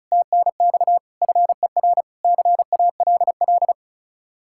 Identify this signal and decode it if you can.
Morse code, heard as a keyed tone, about 47 wpm, spaced as for 41 wpm, 700 Hz. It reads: TNX FER CALL
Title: Speed-Racing Common QSO Phrases - 35wpm